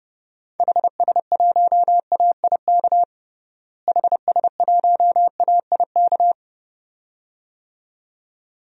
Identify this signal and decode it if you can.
Morse code, heard as a keyed tone, about 32 wpm, 700 Hz. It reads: HS1AIK HS1AIK
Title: DXCC-100 Call Signs - 6 Characters - Repeated 2x - 30wpm